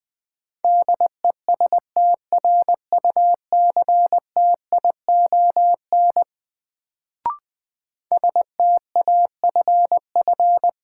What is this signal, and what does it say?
Morse code, heard as a keyed tone, about 20 wpm, 700 Hz. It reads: DESTRUCTION STAFF
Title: Sets of 5 Words Encore - New Word Drill - 20wpm